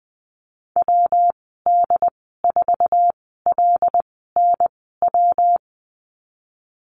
Morse code, heard as a keyed tone, about 20 wpm, spaced as for 15 wpm, 700 Hz. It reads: WD4LNW
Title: 186 Lesson 30g - Letter b - Call Signs 20wpm